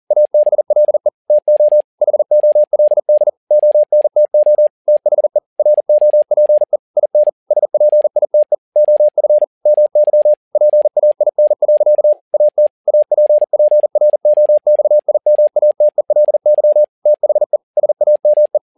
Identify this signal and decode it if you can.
Morse code, heard as a keyed tone, about 40 wpm, 600 Hz. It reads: ABLE TO HOLD ONTO THE ROPE IN SPITE OF MY PAIN. AT APPROXIMATELY THE SAME
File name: learning center / tower trouble / tower touble 40wpm